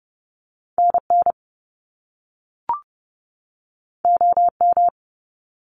Morse code, heard as a keyed tone, about 30 wpm, 700 Hz. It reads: NN OM
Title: Top Two Letter Combinations in Words 30wpm